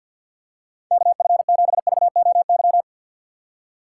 Morse code, heard as a keyed tone, about 53 wpm, 700 Hz. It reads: KF7VOY
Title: USA Call Signs - 6 Characters - 4x Character Spacing - 50wpm